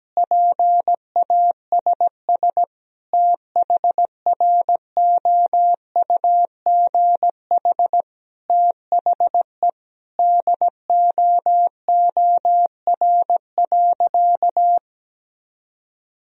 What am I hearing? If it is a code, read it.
Morse code, heard as a keyed tone, about 17 wpm, 700 Hz. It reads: PASS THROUGH THE DOOR.
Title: Sentences from Top 400 Words and Abbreviations 17wpm